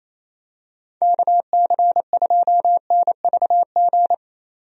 Morse code, heard as a keyed tone, about 28 wpm, 700 Hz. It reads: KC2NVG
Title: USA Call Signs - 6 Characters - 28wpm